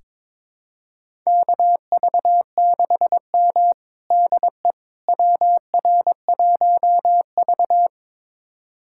Morse code, heard as a keyed tone, about 21 wpm, 700 Hz. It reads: KV6M DE WR1V
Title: USA Callsign DE Callsign - 4 Characters - 22wpm